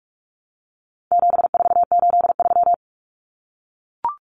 Morse code, heard as a keyed tone, about 44 wpm, 700 Hz. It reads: 7483
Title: Common Cut Numbers - 4-Digit 45wpm